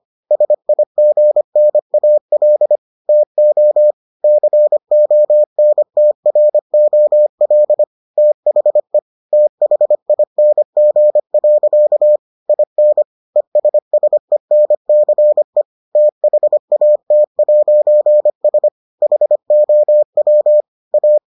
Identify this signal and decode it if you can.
Morse code, heard as a keyed tone, about 25 wpm, 600 Hz. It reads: SIGNAL TO CONTROL THE THING. IN ESSENCE THAT'S HOW A